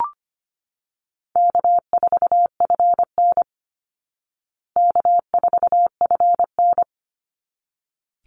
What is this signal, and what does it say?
Morse code, heard as a keyed tone, about 25 wpm, 700 Hz. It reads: K4FN K4FN
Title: Easy Call Signs Repeated 2x - 25wpm